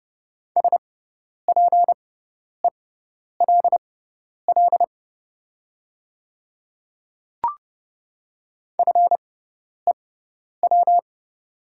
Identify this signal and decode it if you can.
Morse code, heard as a keyed tone, about 30 wpm, spaced as for 10 wpm, 700 Hz. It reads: SPELL FEW
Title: 6x Character Spacing - Top 300 Words - 30wpm